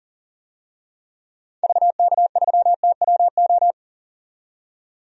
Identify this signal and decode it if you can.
Morse code, heard as a keyed tone, about 40 wpm, 700 Hz. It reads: VK3TWO
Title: DXCC-100 Call Signs - 6 Characters - 40wpm